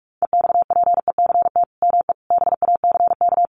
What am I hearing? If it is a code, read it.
Morse code, heard as a keyed tone, about 44 wpm, 700 Hz. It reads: EXPECT ME BACK
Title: Sentences from Top 1100 Words 45wpm